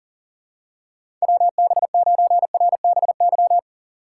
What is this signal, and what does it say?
Morse code, heard as a keyed tone, about 40 wpm, 700 Hz. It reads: WB9RBY
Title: USA Call Signs - 6 Characters - 4x Character Spacing - 40wpm